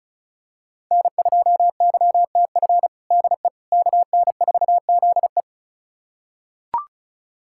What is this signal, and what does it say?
Morse code, heard as a keyed tone, about 34 wpm, 700 Hz. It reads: N2YTF DE KN4ZE
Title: USA Callsign DE Callsign - 5 Characters - 35wpm